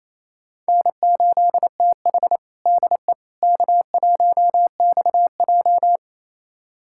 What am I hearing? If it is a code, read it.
Morse code, heard as a keyed tone, about 28 wpm, 700 Hz. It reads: N8TH DE K1XJ
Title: USA Callsign DE Callsign - 4 Characters - 28wpm